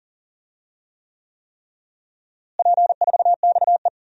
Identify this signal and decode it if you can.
Morse code, heard as a keyed tone, about 37 wpm, 700 Hz. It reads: P4XE
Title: DXCC-100 Call Signs - 4 Characters - Repeated 3x - 40wpm